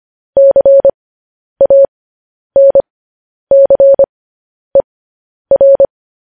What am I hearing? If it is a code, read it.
Morse code, heard as a keyed tone, about 25 wpm, spaced as for 10 wpm, 550 Hz. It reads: CANCER
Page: news headlines in morse code at 10 wpm